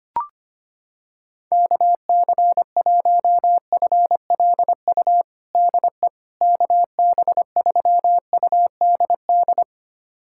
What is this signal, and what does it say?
Morse code, heard as a keyed tone, about 25 wpm, 700 Hz. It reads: KC1FLU DE KB3UDD
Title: USA Callsign DE Callsign - 6 Characters - 25wpm